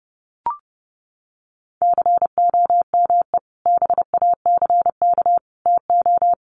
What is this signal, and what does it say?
Morse code, heard as a keyed tone, about 30 wpm, 700 Hz. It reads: COME BACK TO
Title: Sentences from Top 500 Words 30wpm